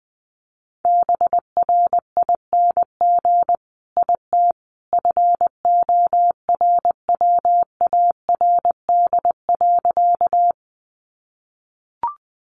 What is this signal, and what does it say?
Morse code, heard as a keyed tone, about 20 wpm, 700 Hz. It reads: BRING IT FORWARD.
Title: Sentences from Top 1300 Words 20wpm